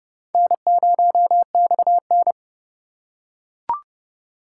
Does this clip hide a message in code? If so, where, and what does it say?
Yes: Morse code from 0.3 to 2.3 s, heard as a keyed tone, about 28 wpm, 700 Hz. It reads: N0XN